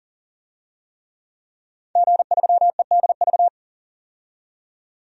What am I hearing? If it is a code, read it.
Morse code, heard as a keyed tone, about 40 wpm, 700 Hz. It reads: G3EDV
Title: DXCC-100 Call Signs - 5 Characters - 40wpm